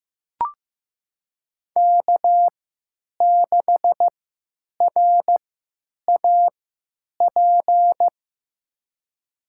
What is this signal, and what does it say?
Morse code, heard as a keyed tone, about 15 wpm, spaced as for 9 wpm, 700 Hz. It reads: K6RAP
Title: USA Call Signs - 5 Characters - 3x Character Spacing - 15wpm